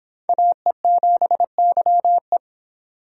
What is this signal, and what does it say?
Morse code, heard as a keyed tone, about 26 wpm, 700 Hz. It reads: AE7YE